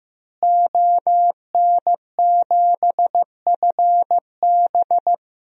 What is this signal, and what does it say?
Morse code, heard as a keyed tone, about 15 wpm, 700 Hz. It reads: ON7FB
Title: DXCC-100 Call Signs - 5 Characters - Repeated 3x - 15wpm